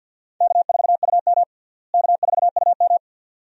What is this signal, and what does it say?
Morse code, heard as a keyed tone, about 48 wpm, 700 Hz. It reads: K4UM K4UM
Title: USA Call Signs - 4 Characters - Repeated 2x - 50wpm